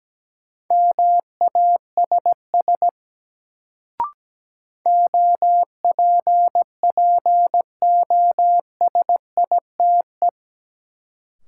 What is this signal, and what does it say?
Morse code, heard as a keyed tone, about 17 wpm, 700 Hz. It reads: MASS OPPOSITE
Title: Sets of 5 Words Encore - New Word Drill - 17wpm